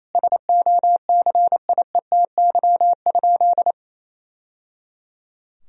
Morse code, heard as a keyed tone, about 27 wpm, 700 Hz. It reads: SOCIETY?
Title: Sentences from Top 1000 Words - Review Part 4 - 28wpm